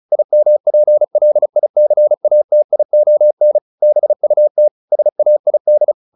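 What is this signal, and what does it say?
Morse code, heard as a keyed tone, about 36 wpm, 600 Hz. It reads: IMPLICATION BUT SAID
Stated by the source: Morse practice at 35 wpm from CW med Gnister, sending Ken Follett's novel Never in English.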